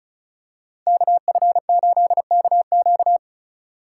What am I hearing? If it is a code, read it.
Morse code, heard as a keyed tone, about 36 wpm, 700 Hz. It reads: KF8KQ
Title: USA Call Signs - 5 Characters - 4x Character Spacing - 35wpm